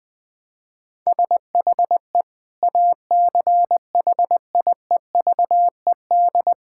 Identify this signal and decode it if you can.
Morse code, heard as a keyed tone, about 20 wpm, 700 Hz. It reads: SHE ACHIEVED
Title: Sentences with 7-Word Phrases - 20wpm